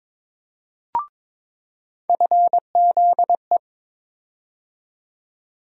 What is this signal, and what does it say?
Morse code, heard as a keyed tone, about 22 wpm, 700 Hz. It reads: FZE
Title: USA Call Sign Postfixes - 3 Letters 22wpm